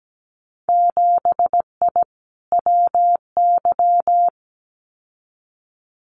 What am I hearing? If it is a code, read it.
Morse code, heard as a keyed tone, about 17 wpm, 700 Hz. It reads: 7I WY
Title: Winter Field Day Exchanges Encore 17wpm